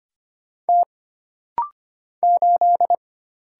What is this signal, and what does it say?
Morse code, heard as a keyed tone, about 25 wpm, 700 Hz. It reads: T 8
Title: Single Letter-Number - Rapid-Fire 25wpm